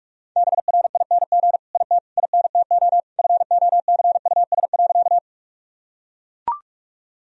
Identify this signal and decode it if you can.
Morse code, heard as a keyed tone, about 42 wpm, 700 Hz. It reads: BRING IT INTO FOCUS.
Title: Sentences from Top 1000 Words - Review Part 4 - 45wpm